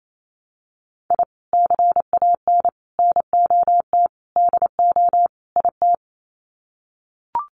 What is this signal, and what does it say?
Morse code, heard as a keyed tone, about 28 wpm, 700 Hz. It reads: I CAN NOT DO IT
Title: Sets of 5 Words Encore 28wpm